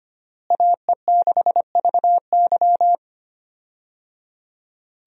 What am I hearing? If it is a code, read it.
Morse code, heard as a keyed tone, about 25 wpm, 700 Hz. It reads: AE6VY
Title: USA Call Signs - 5 Characters - Repeated 3x - 25wpm